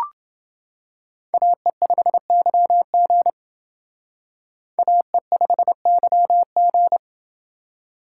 Morse code, heard as a keyed tone, about 28 wpm, 700 Hz. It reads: AE5YG AE5YG